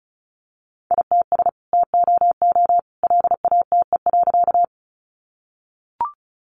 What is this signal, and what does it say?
Morse code, heard as a keyed tone, about 34 wpm, 700 Hz. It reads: ITS TOO LATE.
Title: Sentences from Top 600 Words 35wpm